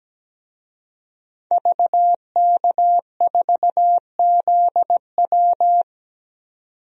Morse code, heard as a keyed tone, about 17 wpm, 700 Hz. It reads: VK4ZW
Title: DXCC-100 Call Signs - 5 Characters - 17wpm